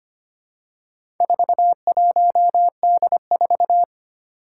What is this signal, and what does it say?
Morse code, heard as a keyed tone, about 25 wpm, 700 Hz. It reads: 41D4
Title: Four Challenging Letter-Number Combinations 25wpm